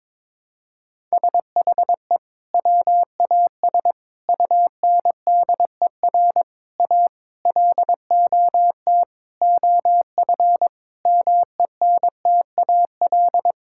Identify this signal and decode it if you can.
Morse code, heard as a keyed tone, about 22 wpm, 700 Hz. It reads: SHE WAS UNDER A LOT OF MENTAL
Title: Sentences from Top 1000 Words - Review Part 6 - 22wpm